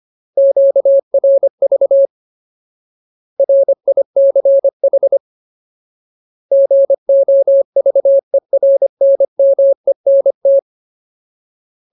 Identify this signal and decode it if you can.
Morse code, heard as a keyed tone, about 25 wpm, 550 Hz. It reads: QRV RICH GOVERNMENT
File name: Ultimate Word List / 25WPM Ultimate